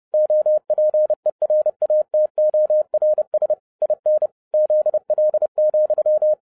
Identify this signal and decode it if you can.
Morse code, heard as a keyed tone, about 30 wpm, 610 Hz. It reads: OPERATORS IN ZL,